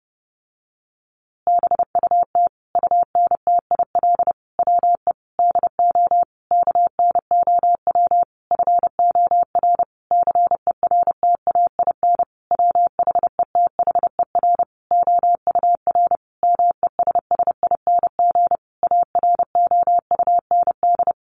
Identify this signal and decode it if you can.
Morse code, heard as a keyed tone, about 30 wpm, 700 Hz. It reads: BUT UNTIL WE DO KNOW FOR CERTAIN WHETHER OUR MESSING AROUND
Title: A Sound Of Thunder 30wpm